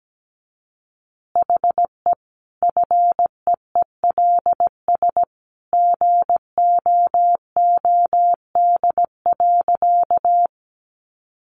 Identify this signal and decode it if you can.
Morse code, heard as a keyed tone, about 17 wpm, 700 Hz. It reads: HE FEELS GOOD.
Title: Additional Easy Phrases 17wpm